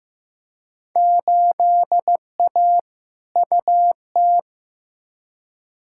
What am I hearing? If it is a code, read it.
Morse code, heard as a keyed tone, about 15 wpm, 700 Hz. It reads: 8A UT
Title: Field Day Exchanges Encore - 15wpm